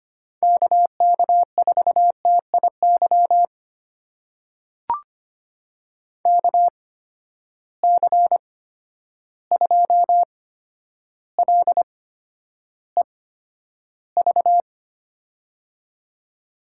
Morse code, heard as a keyed tone, about 25 wpm, spaced as for 11 wpm, 700 Hz. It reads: KK4TIY KC2LEV
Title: USA Call Signs - 6 Characters - 8x Character Spacing - 25wpm